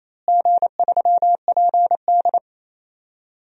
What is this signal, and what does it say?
Morse code, heard as a keyed tone, about 28 wpm, 700 Hz. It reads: G3PD
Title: DXCC-100 Call Signs - 4 Characters - Repeated 2x - 28wpm